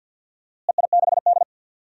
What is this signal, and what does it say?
Morse code, heard as a keyed tone, about 49 wpm, 700 Hz. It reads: EI6D
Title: DXCC-100 Call Signs - 4 Characters - Repeated 2x - 50wpm